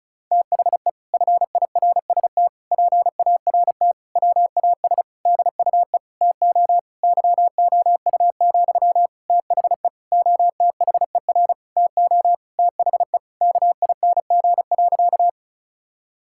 Morse code, heard as a keyed tone, about 34 wpm, 700 Hz. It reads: THE FIRST PART WAS DUE TO YOU, THE OTHER TO THE KING.
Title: Sentences from Top 1000 Words - Review Part 6 - 35wpm